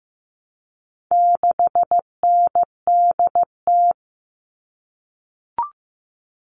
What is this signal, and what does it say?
Morse code, heard as a keyed tone, about 15 wpm, 700 Hz. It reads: 6NDT
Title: All Cut Numbers - 4-Digit 15wpm